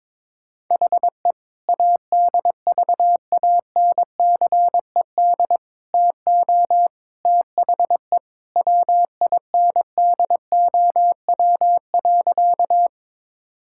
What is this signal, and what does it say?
Morse code, heard as a keyed tone, about 22 wpm, 700 Hz. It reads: HE ADVANCED TO THE WINDOW.
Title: Sentences from Top 1600 Words 22wpm